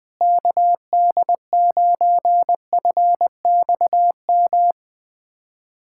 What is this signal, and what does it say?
Morse code, heard as a keyed tone, about 21 wpm, 700 Hz. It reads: KD9FXM
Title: USA Callsign DE Callsign 20wpm